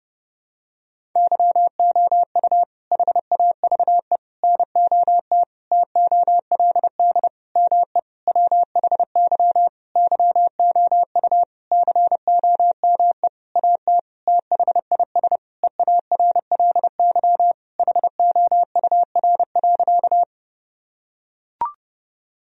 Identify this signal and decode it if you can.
Morse code, heard as a keyed tone, about 30 wpm, 700 Hz. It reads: YOU HAVE NOT TOLD ME WHY YOU COME AT THIS EARLY HOUR.
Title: Sentences from Top 1000 Words - Review Part 2 - 30wpm